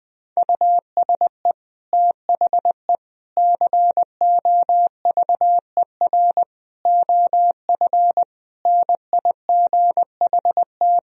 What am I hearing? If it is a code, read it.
Morse code, heard as a keyed tone, about 20 wpm, 700 Hz. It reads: USE THE COVER OF NIGHT
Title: Sentences from Top 1000 Words - Review Part 1 - 20wpm